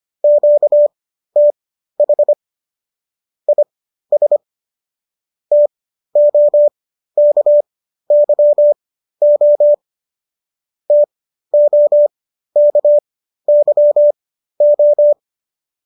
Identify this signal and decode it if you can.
Morse code, heard as a keyed tone, about 25 wpm, spaced as for 13 wpm, 590 Hz. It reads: QTH IS TOKYO TOKYO